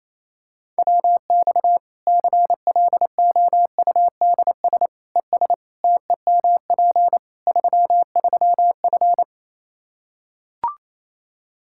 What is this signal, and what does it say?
Morse code, heard as a keyed tone, about 27 wpm, 700 Hz. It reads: WX CLOUDS ES TEMP 33F
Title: K1USN SST QSO Practice 28wpm